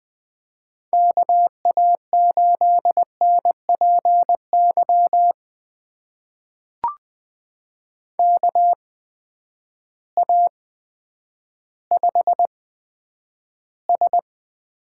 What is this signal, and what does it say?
Morse code, heard as a keyed tone, about 20 wpm, spaced as for 10 wpm, 700 Hz. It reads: KA8NPY KA5S
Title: USA Call Signs - 6 Characters - 8x Character Spacing - 20wpm